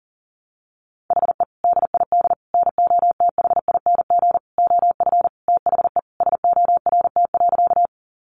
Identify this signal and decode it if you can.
Morse code, heard as a keyed tone, about 40 wpm, 700 Hz. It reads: HE DID NOTHING OF THE SORT.
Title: Sentences from Top 1300 Words 40wpm